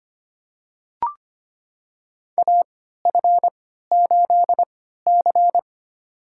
Morse code, heard as a keyed tone, about 25 wpm, spaced as for 14 wpm, 700 Hz. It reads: AF8C
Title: USA Call Signs - 3x Character Spacing - 25wpm